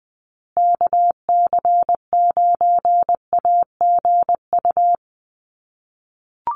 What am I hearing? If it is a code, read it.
Morse code, heard as a keyed tone, about 20 wpm, 700 Hz. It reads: KC9AGU
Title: USA Call Signs Repeated 2x 20wpm